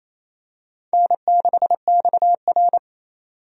Morse code, heard as a keyed tone, about 27 wpm, 700 Hz. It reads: N6XR
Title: USA Call Signs - 4 Characters - 28wpm